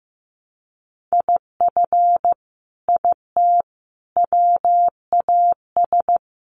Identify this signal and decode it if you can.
Morse code, heard as a keyed tone, about 15 wpm, 700 Hz. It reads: IF IT WAS
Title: Sets of 5 Words Encore 15wpm